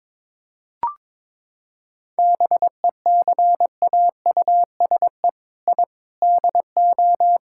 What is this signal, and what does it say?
Morse code, heard as a keyed tone, about 22 wpm, 700 Hz. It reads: BECAUSE I DO
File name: practice / 22wpm / Sentences with 4-Word Phrases - 22wpm